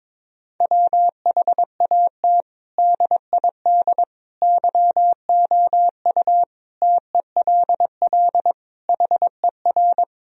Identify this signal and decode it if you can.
Morse code, heard as a keyed tone, about 22 wpm, 700 Hz. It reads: WHAT DID YOU TELL HER
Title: Sentences from Top 1000 Words - Review Part 4 - 22wpm